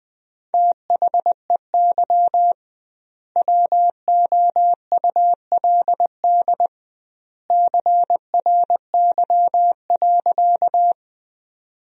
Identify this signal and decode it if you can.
Morse code, heard as a keyed tone, about 20 wpm, 700 Hz. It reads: THEY WOULD CRY.
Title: 2x Word-Spacing - Sentences from Top 500 Words - 20wpm